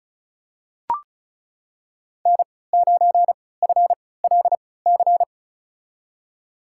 Morse code, heard as a keyed tone, about 36 wpm, spaced as for 20 wpm, 700 Hz. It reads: N9FLC
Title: USA Call Signs - 5 Characters - 3x Character Spacing - 35wpm